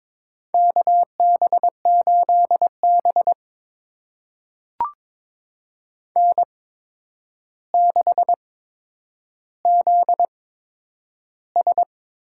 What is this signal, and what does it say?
Morse code, heard as a keyed tone, about 22 wpm, spaced as for 9 wpm, 700 Hz. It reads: KB8B N6ZS
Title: USA Call Signs - 4 Characters - 8x Character Spacing - 22wpm